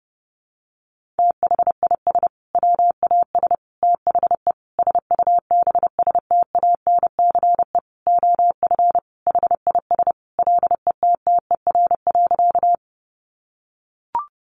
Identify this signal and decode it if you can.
Morse code, heard as a keyed tone, about 30 wpm, 700 Hz. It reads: THIS WAS THE SUBSTANCE OF HIS LETTER.
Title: Sentences from Top 1500 Words 30wpm